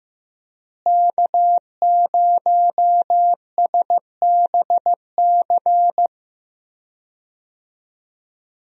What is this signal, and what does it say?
Morse code, heard as a keyed tone, about 15 wpm, 700 Hz. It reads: K0SBC
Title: USA Call Signs - 5 Characters - Repeated 3x - 15wpm